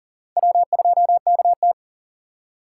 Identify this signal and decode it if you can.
Morse code, heard as a keyed tone, about 40 wpm, 700 Hz. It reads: W2KT
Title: USA Call Signs - 4 Characters - 3x Character Spacing - 40wpm